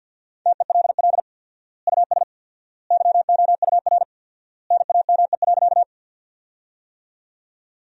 Morse code, heard as a keyed tone, about 46 wpm, 700 Hz. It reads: TELL US YOUR NAME.
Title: 4x Word-Spacing - Sentences from Top 200 Words - 50wpm